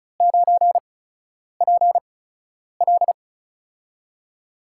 Morse code, heard as a keyed tone, about 36 wpm, spaced as for 10 wpm, 700 Hz. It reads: 9PL
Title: USA Call Signs - 4 Characters - 8x Character Spacing - 35wpm